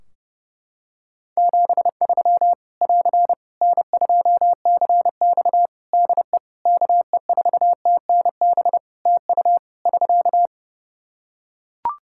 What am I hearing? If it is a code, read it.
Morse code, heard as a keyed tone, about 30 wpm, 700 Hz. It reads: 73 <AR> N2CX DE KE4TNB TU <SK>
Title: Practice QSO - Exchange 4 - Ending - 30wpm